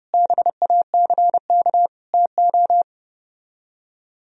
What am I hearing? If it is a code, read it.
Morse code, heard as a keyed tone, about 33 wpm, 700 Hz. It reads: BACK TO